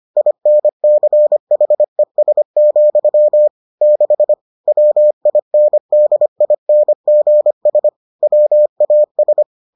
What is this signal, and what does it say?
Morse code, heard as a keyed tone, about 25 wpm, 600 Hz. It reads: INCHES, 6 WINDINGS WAS